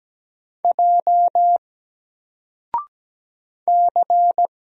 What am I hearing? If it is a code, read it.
Morse code, heard as a keyed tone, about 17 wpm, 700 Hz. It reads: J C
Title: Single Letter-Number Alt 17wpm